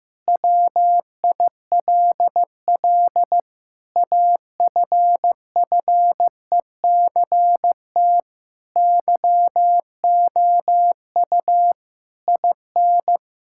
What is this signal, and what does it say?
Morse code, heard as a keyed tone, about 15 wpm, 700 Hz. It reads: WILL AFFECT YOU IN
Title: Sentences from Top 1000 Words - Review Part 6 - 15wpm